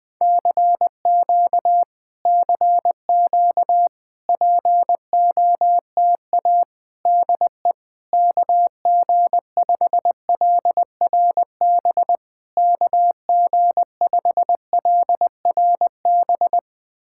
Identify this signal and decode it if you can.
Morse code, heard as a keyed tone, about 20 wpm, 700 Hz. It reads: CQ CQ POTA DE KG5LRB KG5LRB
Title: POTA QSO Practice - 20wpm